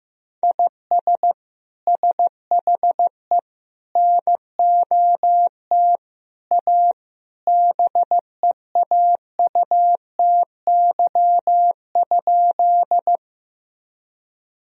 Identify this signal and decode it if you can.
Morse code, heard as a keyed tone, about 15 wpm, 700 Hz. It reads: IS SHE NOT A BEAUTY?
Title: Sentences from Top 1000 Words - Review Part 2 - 15wpm